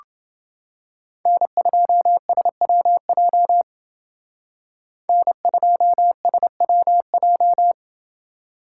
Morse code, heard as a keyed tone, about 28 wpm, 700 Hz. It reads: N2SWJ N2SWJ